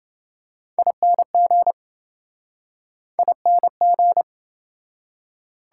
Morse code, heard as a keyed tone, about 28 wpm, 700 Hz. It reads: ING ING